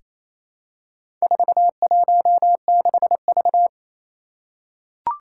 Morse code, heard as a keyed tone, about 28 wpm, 700 Hz. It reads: 416V
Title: Four Challenging Letter-Number Combinations 28wpm